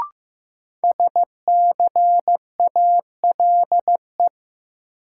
Morse code, heard as a keyed tone, about 15 wpm, 700 Hz. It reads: SCALE
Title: Sentences from Top 1000 Words - Review Part 4 - New Word Drill - 15wpm